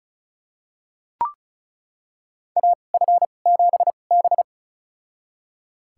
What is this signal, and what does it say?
Morse code, heard as a keyed tone, about 36 wpm, spaced as for 25 wpm, 700 Hz. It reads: AF7B
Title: USA Call Signs - 4 Characters - 2x Character Spacing - 35wpm